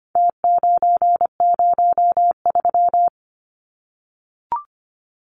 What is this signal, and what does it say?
Morse code, heard as a keyed tone, about 25 wpm, 700 Hz. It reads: T903
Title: Numbers - Contest Style - 4-Digit 25wpm